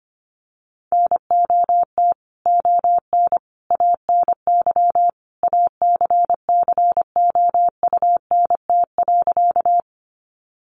Morse code, heard as a keyed tone, about 25 wpm, 700 Hz. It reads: NOT ON ANY ACCOUNT.
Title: Sentences from Top 900 Words 25wpm